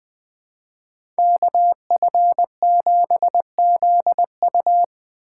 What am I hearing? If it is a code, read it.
Morse code, heard as a keyed tone, about 20 wpm, 700 Hz. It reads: KF7ZU
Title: USA Call Signs - 5 Characters - 4x Character Spacing - 20wpm